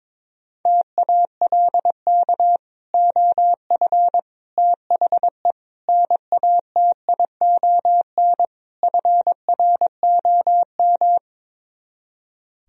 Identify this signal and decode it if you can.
Morse code, heard as a keyed tone, about 22 wpm, 700 Hz. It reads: TALK OF THE NATION FROM
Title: Speed-Racing - Sets of 5 Words 22wpm